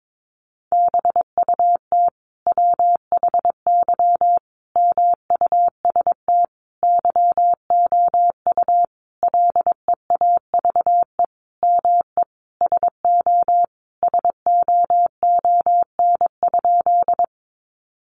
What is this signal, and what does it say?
Morse code, heard as a keyed tone, about 22 wpm, 700 Hz. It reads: BUT WHY MUST YOU LEAVE ME SO SOON?
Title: Sentences from Top 400 Words 22wpm